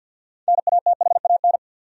Morse code, heard as a keyed tone, about 50 wpm, 700 Hz. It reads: NATHAN
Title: USA Common Names - 50wpm